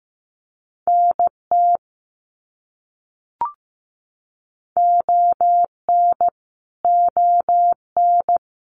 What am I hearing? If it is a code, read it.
Morse code, heard as a keyed tone, about 15 wpm, 700 Hz. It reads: NT ON ON
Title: Canadian Provinces and Territories - Repeated 3x - 15wpm